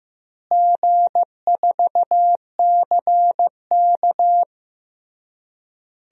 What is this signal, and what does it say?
Morse code, heard as a keyed tone, about 15 wpm, 700 Hz. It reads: G4CK
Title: DXCC-100 Call Signs - 4 Characters - Repeated 3x - 15wpm